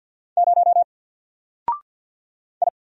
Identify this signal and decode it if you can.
Morse code, heard as a keyed tone, about 49 wpm, 700 Hz. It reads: 0 I
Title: Single Character (Letter:Number:Punctuation:Prosign) Alt 50wpm